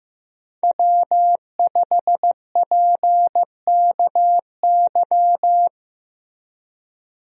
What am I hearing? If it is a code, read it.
Morse code, heard as a keyed tone, about 16 wpm, 700 Hz. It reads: W5PKY
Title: USA Call Signs - 5 Characters - Repeated 3x - 15wpm